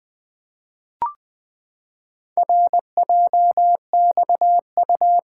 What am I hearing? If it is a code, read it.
Morse code, heard as a keyed tone, about 20 wpm, 700 Hz. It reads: RJXU
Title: Four Letter-Number Combinations - 20wpm